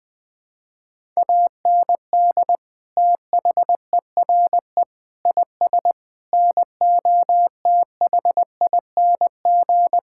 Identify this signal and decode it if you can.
Morse code, heard as a keyed tone, about 20 wpm, 700 Hz. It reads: AND THERE IS NOTHING
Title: Sentences with 8-Word Phrases - Spoken Beginning - 20wpm